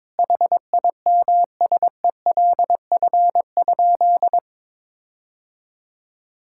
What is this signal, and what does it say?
Morse code, heard as a keyed tone, about 22 wpm, 700 Hz. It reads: HIMSELF?
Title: Sentences from Top 1600 Words 22wpm